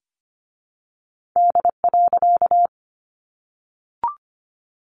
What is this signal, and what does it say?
Morse code, heard as a keyed tone, about 25 wpm, 700 Hz. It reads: D.
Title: A Sound Of Thunder 25wpm